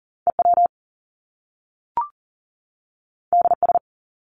Morse code, heard as a keyed tone, about 40 wpm, 700 Hz. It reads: EW DS
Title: Top Two Letter Combinations in Words 40wpm